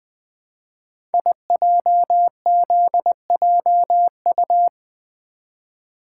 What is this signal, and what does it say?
Morse code, heard as a keyed tone, about 21 wpm, 700 Hz. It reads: IJZJU
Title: Code Groups - 5 Letter:Numbers - 20wpm